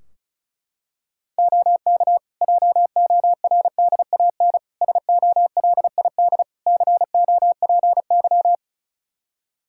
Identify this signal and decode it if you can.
Morse code, heard as a keyed tone, about 36 wpm, 700 Hz. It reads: OK JORDAN SOLID COPY
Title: Practice QSO - Exchange 3 - More Info - 35wpm